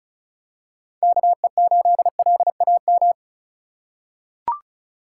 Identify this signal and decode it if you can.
Morse code, heard as a keyed tone, about 36 wpm, 700 Hz. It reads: KE8LAM
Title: USA Call Signs - 6 Characters - 3x Character Spacing - 35wpm